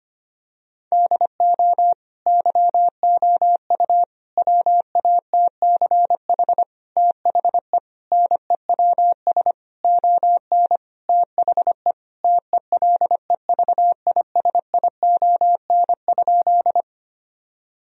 Morse code, heard as a keyed tone, about 25 wpm, 700 Hz. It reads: DO YOU WATCH THE NEWS ON THE TELEVISION?
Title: Sentences from Top 1000 Words - Review Part 4 - 25wpm